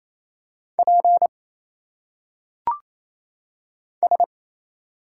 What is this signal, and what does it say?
Morse code, heard as a keyed tone, about 28 wpm, 700 Hz. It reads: P S
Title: Single Letters - Phonetic Pronunciation 28wpm